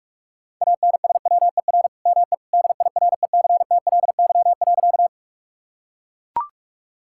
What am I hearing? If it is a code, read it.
Morse code, heard as a keyed tone, about 44 wpm, 690 Hz. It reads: ANSWER ME DIRECTLY.
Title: Sentences from Top 1000 Words - Review Part 5 - 45wpm